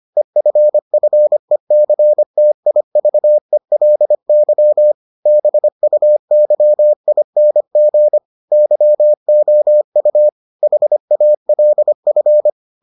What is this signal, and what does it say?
Morse code, heard as a keyed tone, about 25 wpm, 600 Hz. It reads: EFFECTIVELY BUYING YOU HALF